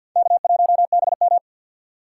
Morse code, heard as a keyed tone, about 49 wpm, 700 Hz. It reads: K1BM
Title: USA Call Signs - 4 Characters - 2x Character Spacing - 50wpm